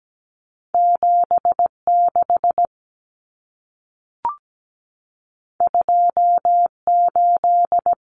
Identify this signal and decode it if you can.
Morse code, heard as a keyed tone, about 17 wpm, 700 Hz. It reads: 76 28
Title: Numbers - 2-Digit 17wpm